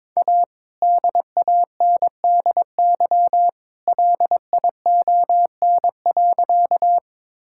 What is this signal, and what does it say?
Morse code, heard as a keyed tone, about 22 wpm, 690 Hz. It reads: A DANDY LION.
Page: morse-code-ninja-practice-22wpm